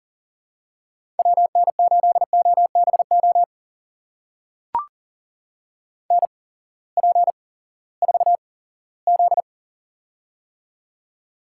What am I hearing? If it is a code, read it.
Morse code, heard as a keyed tone, about 40 wpm, spaced as for 20 wpm, 700 Hz. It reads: WN8OBO NP4Z
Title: USA Call Signs - 8x Character Spacing - 40wpm